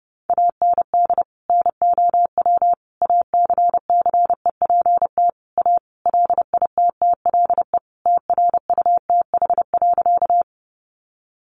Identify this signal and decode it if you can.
Morse code, heard as a keyed tone, about 30 wpm, 700 Hz. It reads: AND NOW ACCEPT A LITTLE TRUTH.
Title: Sentences from Top 1300 Words 30wpm